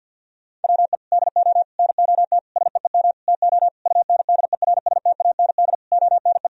Morse code, heard as a keyed tone, about 51 wpm, 700 Hz. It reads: WE DO NOT SEEM TO UNDERSTAND ONE
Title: Sentences from Top 1000 Words - Review Part 1 - 50wpm